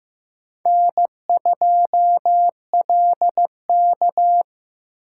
Morse code, heard as a keyed tone, about 15 wpm, 700 Hz. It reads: N2LK
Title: K1USN SST QSO Practice - 15wpm